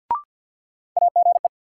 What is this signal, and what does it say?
Morse code, heard as a keyed tone, about 51 wpm, 700 Hz. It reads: AGE
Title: Sentences from Top 500 Words - New Word Drill - 50wpm